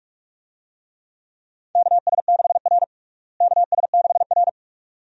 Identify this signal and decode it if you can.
Morse code, heard as a keyed tone, about 41 wpm, 700 Hz. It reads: KS6R KS6R
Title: USA Call Signs - 4 Characters - Repeated 2x - 45wpm